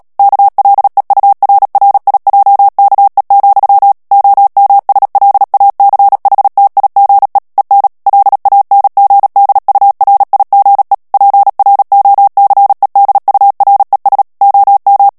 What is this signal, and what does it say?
Morse code, heard as a keyed tone, about 37 wpm, 800 Hz. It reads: KLEURRIJKE, OMSLACHTIGE EN LANGDURIGE PROCEDURES OM
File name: morse mp3 / HOLLANDAIS texte clair 2 vitesse 35 mots minute